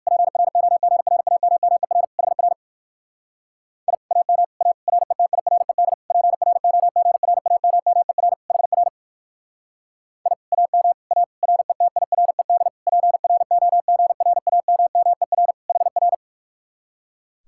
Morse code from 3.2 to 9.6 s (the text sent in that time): I AM A RETIRED PROGRAMMER HR